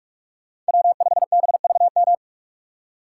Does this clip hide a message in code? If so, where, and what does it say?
Yes: Morse code from 0.7 to 2.1 s, heard as a keyed tone, about 58 wpm, 700 Hz. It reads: W5BVM